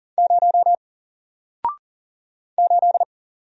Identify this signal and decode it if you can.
Morse code, heard as a keyed tone, about 40 wpm, 700 Hz. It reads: 0 8
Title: Single Character (Letter:Number:Punctuation:Prosign) Alt 40wpm